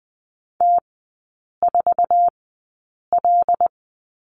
Morse code, heard as a keyed tone, about 20 wpm, spaced as for 8 wpm, 690 Hz. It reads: T4L
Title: 087 Lesson 15e - Number 5 - Three Characters 20wpm